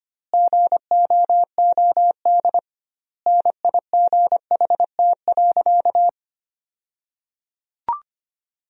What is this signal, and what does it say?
Morse code, heard as a keyed tone, about 25 wpm, 700 Hz. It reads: GOOD NIGHT.
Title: 2x Word-Spacing - Sentences from Top 300 Words - 25wpm